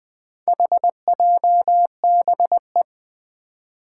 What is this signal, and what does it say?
Morse code, heard as a keyed tone, about 21 wpm, 700 Hz. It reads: HJBE
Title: Four Challenging Letter-Number Combinations 20wpm